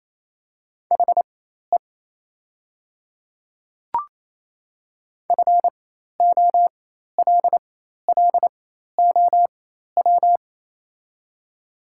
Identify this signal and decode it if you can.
Morse code, heard as a keyed tone, about 28 wpm, spaced as for 13 wpm, 700 Hz. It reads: HE FOLLOW
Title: 4x Character Spacing - Top 200 Words - 28wpm